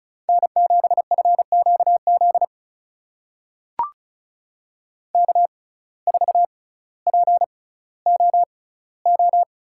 Morse code, heard as a keyed tone, about 36 wpm, spaced as for 18 wpm, 700 Hz. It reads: N7FQZ K4POO
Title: USA Call Signs - 5 Characters - 6x Character Spacing - 35wpm